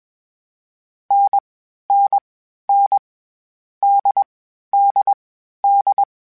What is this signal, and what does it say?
Morse code, heard as a keyed tone, about 21 wpm, 800 Hz. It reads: N N N D D D
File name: Short Runs / Short Runs-mp3s / NDB NDB6 @20